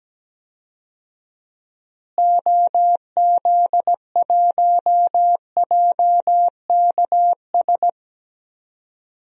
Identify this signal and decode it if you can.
Morse code, heard as a keyed tone, about 17 wpm, 700 Hz. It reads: OZ1JKS
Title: DXCC-100 Call Signs - 6 Characters - Repeated 2x - 17wpm